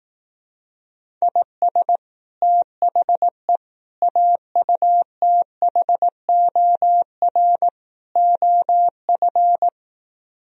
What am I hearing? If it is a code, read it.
Morse code, heard as a keyed tone, about 18 wpm, 700 Hz. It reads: IS THE AUTHOR OF
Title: Speed-Racing - Sets of 4 Words 15wpm